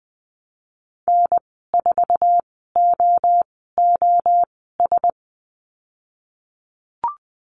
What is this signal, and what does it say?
Morse code, heard as a keyed tone, about 20 wpm, spaced as for 15 wpm, 700 Hz. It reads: N4OOS
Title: 055 Lesson 10g - Letter r - Call Signs 20wpm